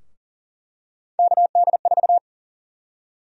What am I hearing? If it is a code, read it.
Morse code, heard as a keyed tone, about 40 wpm, 700 Hz. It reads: KD4
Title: USA Call Sign Prefixes - Repeated 2x - 40wpm